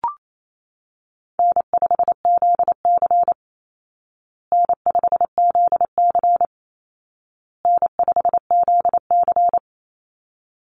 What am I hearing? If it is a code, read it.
Morse code, heard as a keyed tone, about 28 wpm, 700 Hz. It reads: N5ZC N5ZC N5ZC
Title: Easy Call Signs Repeated 3x - 28wpm